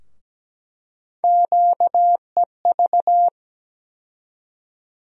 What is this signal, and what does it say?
Morse code, heard as a keyed tone, about 17 wpm, 700 Hz. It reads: QEV
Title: USA Call Sign Postfixes - 3 Letters 17wpm